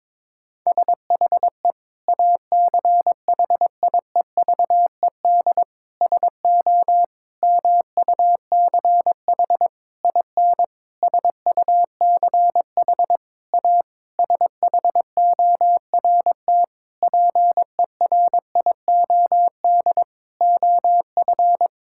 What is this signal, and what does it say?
Morse code, heard as a keyed tone, about 22 wpm, 700 Hz. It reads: SHE ACHIEVED SO MUCH IN SUCH A SHORT PERIOD OF
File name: Sentences with 7-Word Phrases - 22wpm